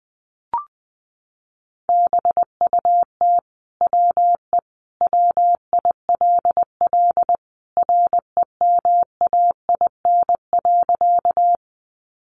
Morse code, heard as a keyed tone, about 20 wpm, 700 Hz. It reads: BUT WE WILL REMAIN.
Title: Sentences from Top 1200 Words 20wpm